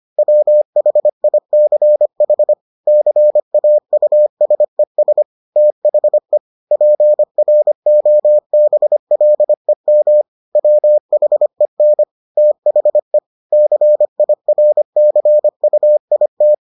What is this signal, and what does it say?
Morse code, heard as a keyed tone, about 25 wpm, 600 Hz. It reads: WHICH CAUSES THE PROBLEM WHEN THE CIRCUIT